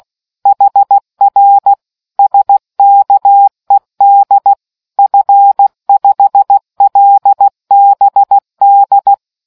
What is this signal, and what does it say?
Morse code, heard as a keyed tone, about 16 wpm, 800 Hz. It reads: HR SKED F5LBD